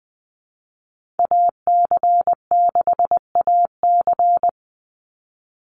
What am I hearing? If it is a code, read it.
Morse code, heard as a keyed tone, about 20 wpm, 700 Hz. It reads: AC6AC
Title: Call Signs Repeated 3x - 20wpm